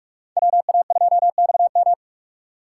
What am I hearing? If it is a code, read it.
Morse code, heard as a keyed tone, about 39 wpm, 700 Hz. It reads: WA2XM